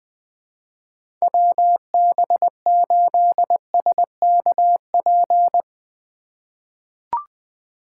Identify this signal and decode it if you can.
Morse code, heard as a keyed tone, about 20 wpm, 700 Hz. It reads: WB8SKP
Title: K1USN SST QSO Practice - 20wpm